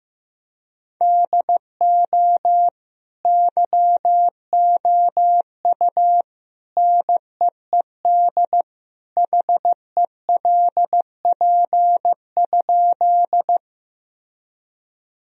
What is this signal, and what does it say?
Morse code, heard as a keyed tone, about 15 wpm, 700 Hz. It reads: DO YOU NEED HELP?
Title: Sentences from Top 1000 Words - Review Part 1 - 15wpm